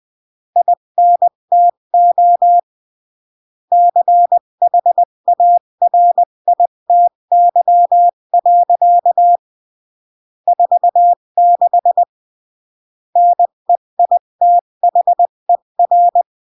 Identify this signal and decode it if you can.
Morse code, heard as a keyed tone, about 20 wpm, 700 Hz. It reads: INTO CHARITY. 46 NEITHER
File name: CW / jodorowsky / 82 commandments 20-15